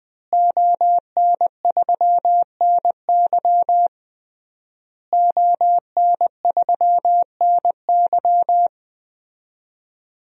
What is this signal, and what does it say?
Morse code, heard as a keyed tone, about 20 wpm, 700 Hz. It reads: ON3NY ON3NY
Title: DXCC-100 Call Signs - 5 Characters - Repeated 3x - 20wpm